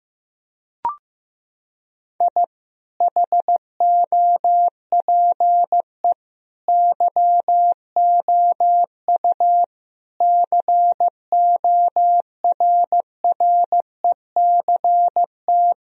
Morse code, heard as a keyed tone, about 15 wpm, 700 Hz. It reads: I HOPE YOU CORRECT
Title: Sentences from Top 1000 Words - Review Part 2 - 15wpm